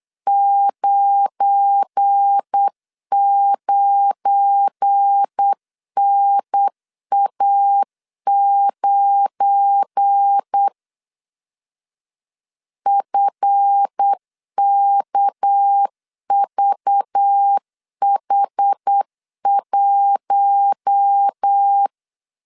Morse code, heard as a keyed tone, about 8 wpm, 800 Hz. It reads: 99NA9 FKVH1
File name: LEZIONI / 7 GF9Z / 7 GF9 8wpm 3545